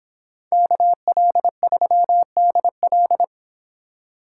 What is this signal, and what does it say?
Morse code, heard as a keyed tone, about 30 wpm, 700 Hz. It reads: KL3DL